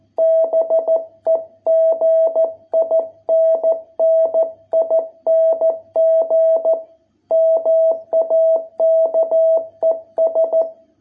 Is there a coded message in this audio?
Yes: Morse code from 0.2 to 10.6 s, heard as a keyed tone, about 14 wpm, 660 Hz. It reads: BEGINNING MAKES